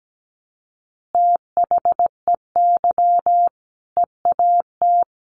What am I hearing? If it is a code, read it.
Morse code, heard as a keyed tone, about 17 wpm, 700 Hz. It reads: THEY EAT
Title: Easy Phrases 17wpm